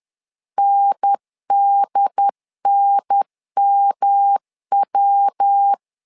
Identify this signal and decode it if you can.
Morse code, heard as a keyed tone, about 10 wpm, 800 Hz. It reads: NDNMW
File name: LEZIONI / 3 MWRO / 3 MW 10wpm 1959